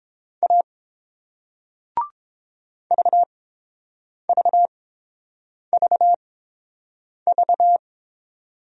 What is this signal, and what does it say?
Morse code, heard as a keyed tone, about 30 wpm, spaced as for 7 wpm, 690 Hz. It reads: A VVVV